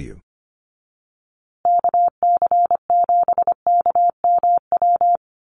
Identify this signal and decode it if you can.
Morse code, heard as a keyed tone, about 25 wpm, 700 Hz. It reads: KC7KMW
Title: USA Call Signs Repeated 3x 25wpm